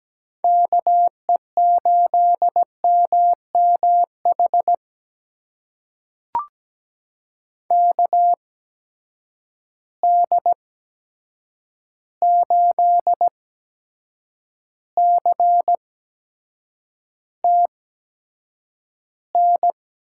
Morse code, heard as a keyed tone, about 17 wpm, spaced as for 7 wpm, 700 Hz. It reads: KE8MMH KD8CTN
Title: USA Call Signs - 6 Characters - 8x Character Spacing - 17wpm